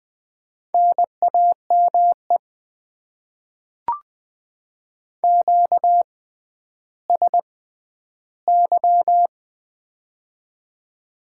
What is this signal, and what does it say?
Morse code, heard as a keyed tone, about 20 wpm, spaced as for 11 wpm, 700 Hz. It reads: NAME QSY
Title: Top 90 QSO Elements - 6x Character Spacing - 20wpm